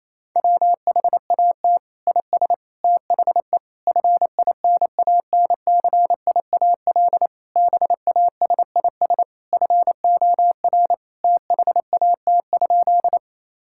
Morse code, heard as a keyed tone, about 28 wpm, 700 Hz. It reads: WHAT IS THE FINANCIAL BASIS FOR THAT?
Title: Sentences from Top 1000 Words - Review Part 5 - 28wpm